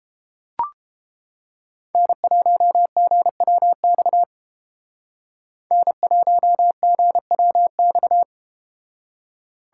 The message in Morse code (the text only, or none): N1GWX N1GWX